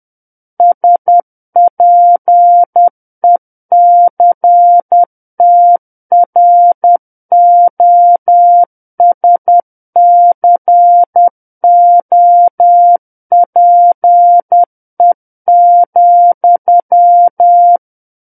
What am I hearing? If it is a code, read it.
Morse code, heard as a keyed tone, about 10 wpm, 700 Hz. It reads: SPECTROSCOPE,
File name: Book One / 01-Chapter 1 - 10 WPM